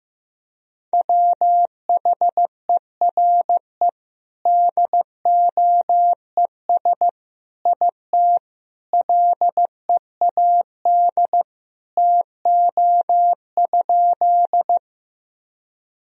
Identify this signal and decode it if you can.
Morse code, heard as a keyed tone, about 16 wpm, 700 Hz. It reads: WHERE DOES IT LEAD TO?
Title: Sentences from Top 1000 Words - Review Part 2 - 15wpm